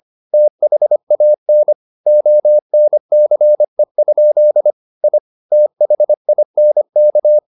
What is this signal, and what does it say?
Morse code, heard as a keyed tone, about 25 wpm, 600 Hz. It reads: THAN ONCE? I THINK
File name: the old man and the sea part 1 25wpm